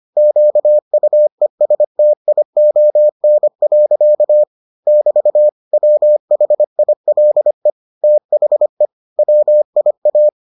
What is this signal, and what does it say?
Morse code, heard as a keyed tone, about 25 wpm, 600 Hz. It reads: QUESTION. <BT> WHILE THE WIA